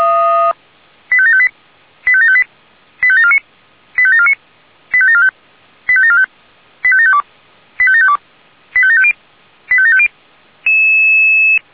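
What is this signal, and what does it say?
Five-tone selective call (ZVEI), heard as quick runs of five tones, tones of about 80 ms. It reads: 85758 85758 85749 85749 85745 85745 85752 85752 85750 85750
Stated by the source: Five-tone ZVEI selective call, received with an SDR.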